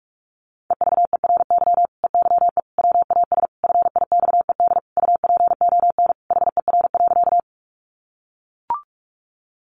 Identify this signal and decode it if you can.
Morse code, heard as a keyed tone, about 44 wpm, 700 Hz. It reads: EVERY EYE WAS FIXED UPON HER.
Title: Sentences from Top 1100 Words 45wpm